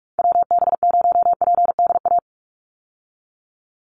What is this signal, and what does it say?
Morse code, heard as a keyed tone, about 44 wpm, 700 Hz. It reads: WB0PDA
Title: USA Call Signs 45wpm